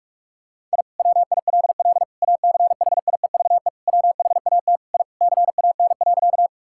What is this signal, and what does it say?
Morse code, heard as a keyed tone, about 46 wpm, 700 Hz. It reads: I WILL ACHIEVE WHAT I CAN.
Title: Sentences from Top 1000 Words - Review Part 6 - 45wpm